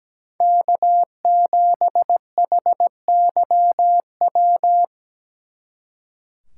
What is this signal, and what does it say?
Morse code, heard as a keyed tone, about 20 wpm, 700 Hz. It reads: K7HYW
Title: Speed Racing - USA Call Signs 17wpm